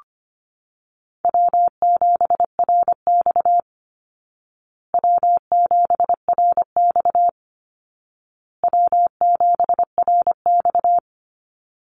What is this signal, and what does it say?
Morse code, heard as a keyed tone, about 25 wpm, 700 Hz. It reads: W7RX W7RX W7RX
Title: Easy Call Signs Repeated 3x - 25wpm